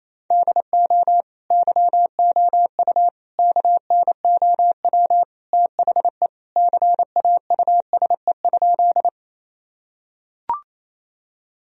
Morse code, heard as a keyed tone, about 28 wpm, 700 Hz. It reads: DO YOU KNOW THE CAUSE?
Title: Sentences from Top 1000 Words - Review Part 1 - 28wpm